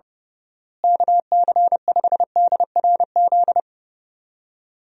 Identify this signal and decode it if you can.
Morse code, heard as a keyed tone, about 30 wpm, 700 Hz. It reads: KC5DRZ
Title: USA Call Signs - 6 Characters - Repeated 2x - 30wpm